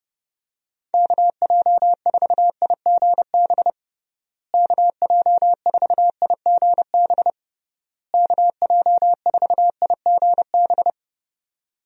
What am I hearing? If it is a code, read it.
Morse code, heard as a keyed tone, about 31 wpm, 700 Hz. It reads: KJ4IGB KJ4IGB KJ4IGB
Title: USA Call Signs - 6 Characters - Repeated 3x - 30wpm